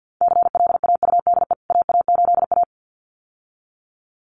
Morse code, heard as a keyed tone, about 48 wpm, 700 Hz. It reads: CLAUDE AA8A
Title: CWT Exchange 50wpm